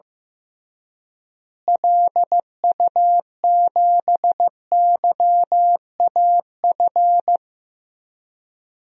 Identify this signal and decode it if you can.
Morse code, heard as a keyed tone, about 15 wpm, 700 Hz. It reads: LU7YAF
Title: DXCC-100 Call Signs - 6 Characters - Repeated 3x - 15wpm